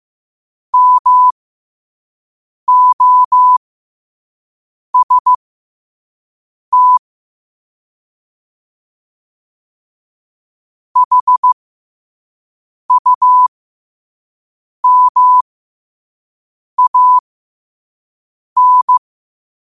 Morse code, heard as a keyed tone, about 15 wpm, spaced as for 5 wpm, 1000 Hz. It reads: MOST HUMAN
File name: qotdfiles / 5wpm